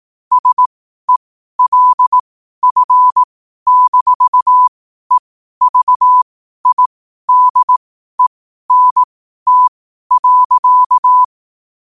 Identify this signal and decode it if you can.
Morse code, heard as a keyed tone, about 18 wpm, spaced as for 12 wpm, 1000 Hz. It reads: SELF-EVIDENT.
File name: qotdfiles / 12wpm